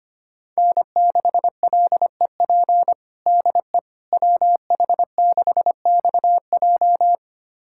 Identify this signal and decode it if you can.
Morse code, heard as a keyed tone, about 25 wpm, 700 Hz. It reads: N6LEP DE WH6XJ
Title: USA Callsign DE Callsign - 5 Characters - 25wpm